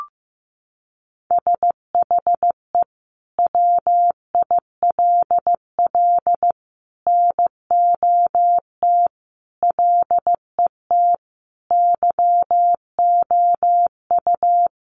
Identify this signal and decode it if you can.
Morse code, heard as a keyed tone, about 15 wpm, 700 Hz. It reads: SHE WILL NOT LET YOU
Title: Sentences from Top 1500 Words 15wpm